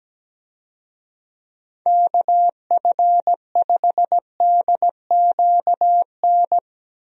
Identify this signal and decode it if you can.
Morse code, heard as a keyed tone, about 17 wpm, 700 Hz. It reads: KF5DQN
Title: USA Call Signs - 6 Characters - Repeated 2x - 17wpm